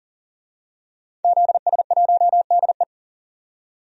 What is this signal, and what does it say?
Morse code, heard as a keyed tone, about 40 wpm, 700 Hz. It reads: ZS1DE
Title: DXCC-100 Call Signs - 5 Characters - 40wpm